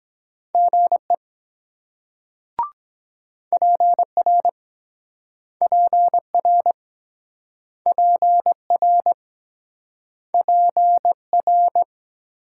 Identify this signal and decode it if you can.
Morse code, heard as a keyed tone, about 23 wpm, 700 Hz. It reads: GE PR PR PR PR